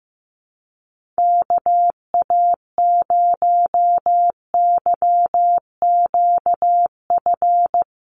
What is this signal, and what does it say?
Morse code, heard as a keyed tone, about 15 wpm, 700 Hz. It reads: KA0YQF
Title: USA Call Signs Repeated 2x 15wpm